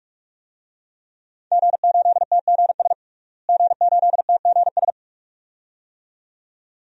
Morse code, heard as a keyed tone, about 42 wpm, 700 Hz. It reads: G8TGS G8TGS
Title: DXCC-100 Call Signs - 5 Characters - Repeated 2x - 45wpm